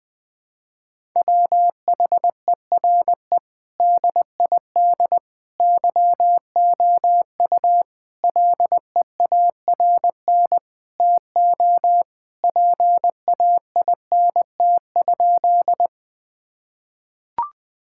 Morse code, heard as a keyed tone, about 20 wpm, 700 Hz. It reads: WHERE DID YOU LEARN TO PAINT?
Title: Sentences from Top 1000 Words - Review Part 6 - 20wpm